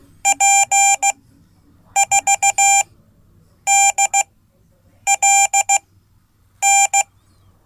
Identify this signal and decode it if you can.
Morse code, heard as a keyed tone, about 15 wpm, spaced as for 8 wpm, 790 Hz. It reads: P4DLN
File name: Code / 5min-b